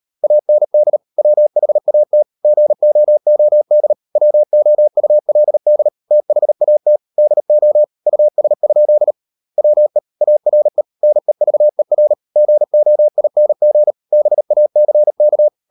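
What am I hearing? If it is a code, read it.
Morse code, heard as a keyed tone, about 40 wpm, 600 Hz. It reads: AND WHAT GOOD WOULD THAT DO US? WE ARE NEVER GOING BACK
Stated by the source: Morse code practice at 35–39 wpm sending English text of Ken Follett's novel Never, from CW med Gnister.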